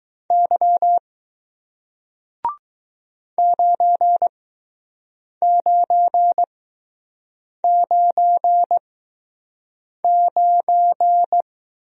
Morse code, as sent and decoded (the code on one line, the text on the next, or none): -.-- / ----. ----. ----. ----.
Y 9999